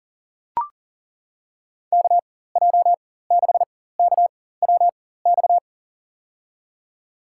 Morse code, heard as a keyed tone, about 40 wpm, spaced as for 19 wpm, 700 Hz. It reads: KJ6KWX
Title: USA Call Signs - 6 Characters - 4x Character Spacing - 40wpm